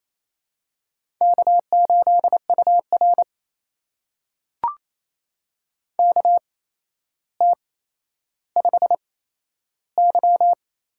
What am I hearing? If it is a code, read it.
Morse code, heard as a keyed tone, about 28 wpm, spaced as for 12 wpm, 700 Hz. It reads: K8UR KT5Y
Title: USA Call Signs - 4 Characters - 8x Character Spacing - 28wpm